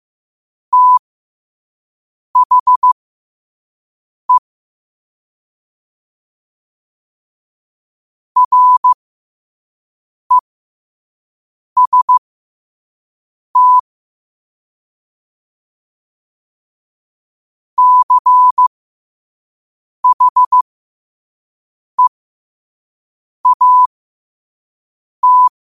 Morse code, heard as a keyed tone, about 15 wpm, spaced as for 5 wpm, 1000 Hz. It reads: THE REST CHEAT
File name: qotdfiles / 5wpm